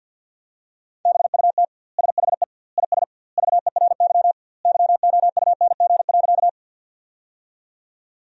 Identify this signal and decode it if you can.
Morse code, heard as a keyed tone, about 48 wpm, 700 Hz. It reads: BUT SHE IS VERY YOUNG.
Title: 2x Word-Spacing - Sentences from Top 400 Words - 50wpm